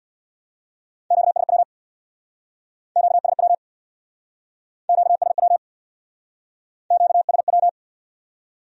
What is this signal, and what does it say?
Morse code, heard as a keyed tone, about 62 wpm, 700 Hz. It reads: QSW QSW QSW QSW
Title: Speed-Racing Q-Codes - 50wpm